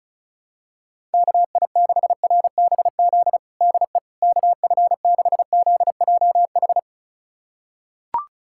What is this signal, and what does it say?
Morse code, heard as a keyed tone, about 34 wpm, 700 Hz. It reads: KI6RBZ DE KF6ZJH
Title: USA Callsign DE Callsign - 6 Characters - 35wpm